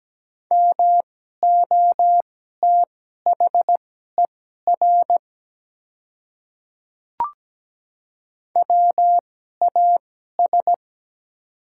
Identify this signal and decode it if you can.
Morse code, heard as a keyed tone, about 17 wpm, spaced as for 12 wpm, 690 Hz. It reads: MOTHER WAS
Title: 2x Character Spacing - Top 300 Words - 17wpm